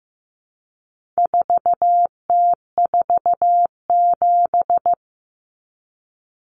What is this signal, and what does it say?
Morse code, heard as a keyed tone, about 15 wpm, 700 Hz. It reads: 4T47
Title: Common Cut Numbers - 4-Digit 15wpm